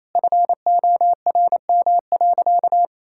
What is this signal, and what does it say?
Morse code, heard as a keyed tone, about 28 wpm, 700 Hz. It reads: FORM.
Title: Sentences from Top 1000 Words - Review Part 5 - 28wpm